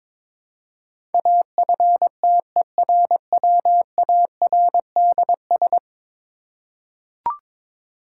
Morse code, heard as a keyed tone, about 22 wpm, 700 Hz. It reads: AFTERWARDS
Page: morse-code-ninja-practice-22wpm